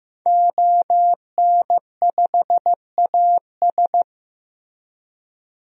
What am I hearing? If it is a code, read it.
Morse code, heard as a keyed tone, about 15 wpm, 700 Hz. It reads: ON5AS
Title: DXCC-100 Call Signs - 5 Characters - 15wpm